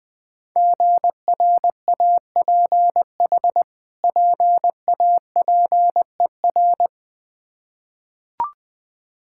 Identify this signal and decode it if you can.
Morse code, heard as a keyed tone, about 21 wpm, 700 Hz. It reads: GRAPH PAPER
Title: Difficult Words with Hints - 20wpm